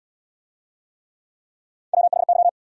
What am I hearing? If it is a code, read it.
Morse code, heard as a keyed tone, about 79 wpm, 700 Hz. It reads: WHY